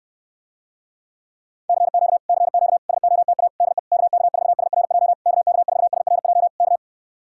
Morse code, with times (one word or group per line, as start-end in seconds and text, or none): 1.7-2.2 CQ
2.3-2.8 CQ
2.9-3.5 SOTA
3.6-3.8 DE
3.9-5.1 KK4SRJ
5.3-6.5 KK4SRJ
6.6-6.8 K